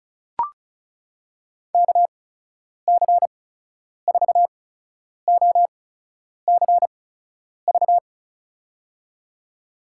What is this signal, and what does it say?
Morse code, heard as a keyed tone, about 36 wpm, spaced as for 10 wpm, 700 Hz. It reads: KC4OCV
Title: USA Call Signs - 6 Characters - 8x Character Spacing - 35wpm